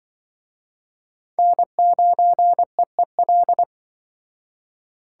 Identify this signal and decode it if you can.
Morse code, heard as a keyed tone, about 25 wpm, 700 Hz. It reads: N9EEL
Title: Speed Racing - USA 1x3 Call Signs 20wpm